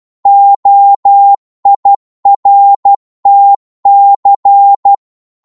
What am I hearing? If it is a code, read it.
Morse code, heard as a keyed tone, about 12 wpm, 800 Hz. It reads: OIRTC
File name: Morse / ltr-12wpm